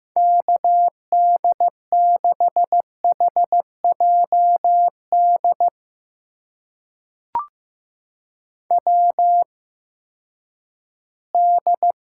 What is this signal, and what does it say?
Morse code, heard as a keyed tone, about 15 wpm, spaced as for 10 wpm, 700 Hz. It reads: KD6HJD WD
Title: USA Call Signs - 6 Characters - 8x Character Spacing - 15wpm